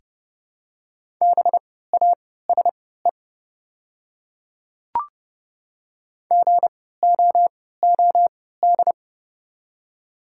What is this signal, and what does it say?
Morse code, heard as a keyed tone, about 30 wpm, spaced as for 17 wpm, 690 Hz. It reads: BASE GOOD
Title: 3x Character Spacing - Top 300 Words - 30wpm